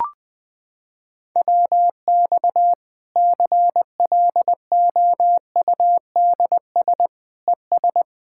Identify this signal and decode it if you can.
Morse code, heard as a keyed tone, about 21 wpm, 700 Hz. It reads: WX CLOUDS ES
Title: K1USN SST QSO Practice 20wpm